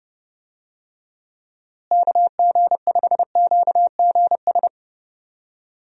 Morse code, heard as a keyed tone, about 30 wpm, 700 Hz. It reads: KG5QGS
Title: USA Call Signs - 6 Characters - Repeated 3x - 30wpm